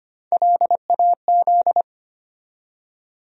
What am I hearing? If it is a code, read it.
Morse code, heard as a keyed tone, about 25 wpm, 700 Hz. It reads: LAZ
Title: USA Call Sign Postfixes - 3 Letters 25wpm